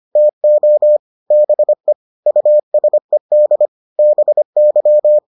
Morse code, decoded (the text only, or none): TO BE USED BY